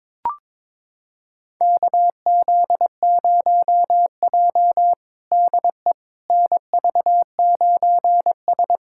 Morse code, heard as a keyed tone, about 22 wpm, 700 Hz. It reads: KZ0J DE NV9S
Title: USA Callsign DE Callsign - 4 Characters - 22wpm